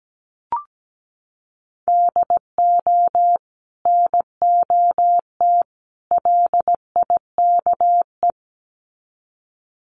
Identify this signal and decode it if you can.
Morse code, heard as a keyed tone, about 17 wpm, 700 Hz. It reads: DO NOT LIKE
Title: Sets of 3 Words 17wpm